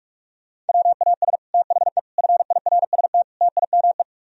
Morse code, heard as a keyed tone, about 60 wpm, spaced as for 52 wpm, 700 Hz. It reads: WAS THE FIRST TIME
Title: Speed-Racing - Sets of 4 Words 45wpm